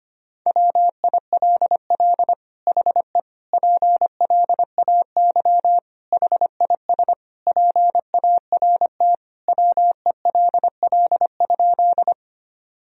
Morse code, heard as a keyed tone, about 25 wpm, 700 Hz. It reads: WILL HE PLAY HIS PART WELL?
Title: Sentences from Top 1000 Words - Review Part 1 - 25wpm